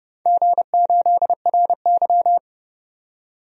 Morse code, heard as a keyed tone, about 31 wpm, 700 Hz. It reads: G8RY
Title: DXCC-100 Call Signs - 4 Characters - 30wpm